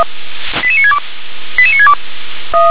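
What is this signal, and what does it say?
Five-tone selective call (ZVEI), heard as quick runs of five tones, tones of about 70 ms. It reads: 88962 88962
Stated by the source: ZVEI five-tone selective call received by SDR.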